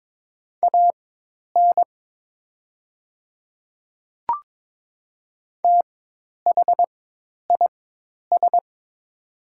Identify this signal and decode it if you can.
Morse code, heard as a keyed tone, about 22 wpm, spaced as for 10 wpm, 700 Hz. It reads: AN THIS
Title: 4x Character Spacing - Top 100 Words - 22wpm